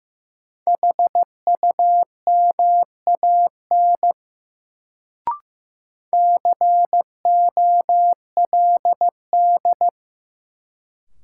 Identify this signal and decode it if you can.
Morse code, heard as a keyed tone, about 15 wpm, 700 Hz. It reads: HUMAN COLD
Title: Sets of 5 Words - New Word Drill - 15wpm